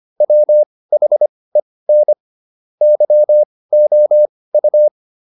Morse code, heard as a keyed tone, about 25 wpm, spaced as for 18 wpm, 600 Hz. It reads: WHEN YOU